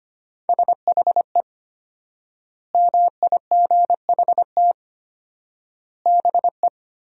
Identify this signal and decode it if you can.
Morse code, heard as a keyed tone, about 25 wpm, 700 Hz. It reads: SHE MIGHT BE
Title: 4x Word-Spacing - Sentences from Top 300 Words - 25wpm